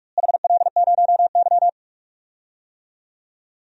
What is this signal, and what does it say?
Morse code, heard as a keyed tone, about 46 wpm, 700 Hz. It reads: HL0Y
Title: DXCC-100 Call Signs - 4 Characters - Repeated 2x - 45wpm